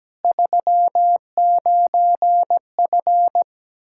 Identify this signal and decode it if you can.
Morse code, heard as a keyed tone, about 17 wpm, 700 Hz. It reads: 39F
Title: Practice QSO - Exchange 2 - Info - 17wpm